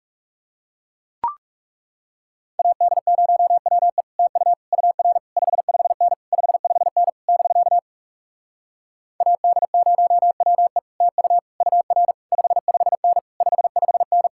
Morse code, heard as a keyed tone, about 37 wpm, 700 Hz. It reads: AD0WE TU UR 55N 55N <BK> AD0WE TU UR 55N 55N
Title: Speed-Racing Common QSO Phrases Encore - 30wpm